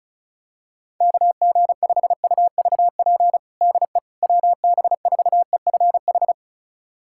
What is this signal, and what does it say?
Morse code, heard as a keyed tone, about 34 wpm, 700 Hz. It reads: KG5UVP DE WB4EFH
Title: USA Callsign DE Callsign 35wpm